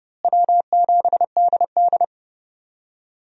Morse code, heard as a keyed tone, about 30 wpm, 700 Hz. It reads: W7DD
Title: USA Call Signs - 4 Characters - Repeated 3x - 30wpm